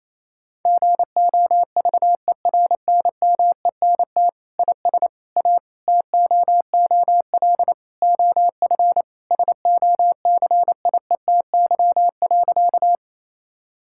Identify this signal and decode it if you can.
Morse code, heard as a keyed tone, about 28 wpm, 700 Hz. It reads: GOVERNMENT IS A TOOL OF SOCIETY.
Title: Sentences from Top 1000 Words - Review Part 6 - 28wpm